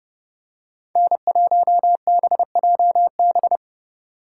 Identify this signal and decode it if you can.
Morse code, heard as a keyed tone, about 31 wpm, 700 Hz. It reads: N1BJB